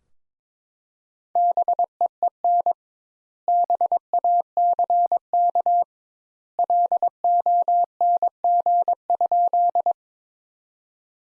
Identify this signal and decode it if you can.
Morse code, heard as a keyed tone, about 22 wpm, 700 Hz. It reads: BEEN BACK LONG?
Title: Sentences Top 200 Words - 22wpm